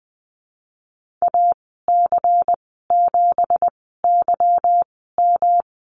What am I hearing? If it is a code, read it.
Morse code, heard as a keyed tone, about 19 wpm, spaced as for 14 wpm, 700 Hz. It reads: AC7YM
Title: 179 Lesson 29g - Slash - Call Signs 20wpm